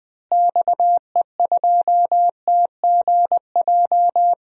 Morse code, heard as a keyed tone, about 20 wpm, 700 Hz. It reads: XE2TGJ
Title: DXCC-100 Call Signs - 6 Characters - Repeated 2x - 20wpm